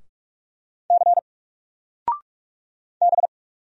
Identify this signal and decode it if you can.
Morse code, heard as a keyed tone, about 44 wpm, 700 Hz. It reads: C B
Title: Single Letter-Number Alt 45wpm